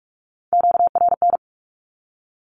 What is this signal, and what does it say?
Morse code, heard as a keyed tone, about 44 wpm, 700 Hz. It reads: QRN
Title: Q-Codes 45wpm